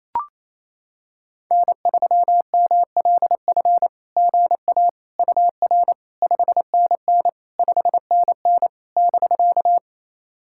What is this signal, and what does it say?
Morse code, heard as a keyed tone, about 28 wpm, 700 Hz. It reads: N3MLF GA UR 5NN 5NN <BK>
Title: SOTA QSO Practice 28wpm